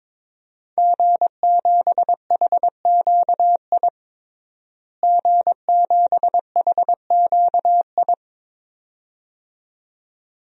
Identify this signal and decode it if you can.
Morse code, heard as a keyed tone, about 21 wpm, 700 Hz. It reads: G7HQI G7HQI
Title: DXCC-100 Call Signs - 5 Characters - Repeated 2x - 22wpm